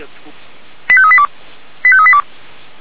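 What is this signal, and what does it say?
Five-tone selective call (ZVEI), heard as quick runs of five tones, tones of about 70 ms, with speech. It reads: 85382 85382
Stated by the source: ZVEI five-tone selective call received on an SDR.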